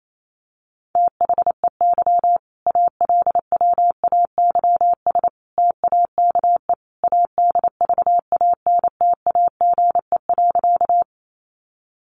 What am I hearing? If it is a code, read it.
Morse code, heard as a keyed tone, about 28 wpm, 700 Hz. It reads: THEY ALWAYS TAKE ADVANTAGE.
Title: Sentences from Top 900 Words 28wpm